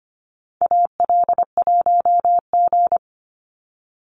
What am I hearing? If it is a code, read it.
Morse code, heard as a keyed tone, about 25 wpm, 700 Hz. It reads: AL1G
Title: Call Signs Repeated 2x - 25wpm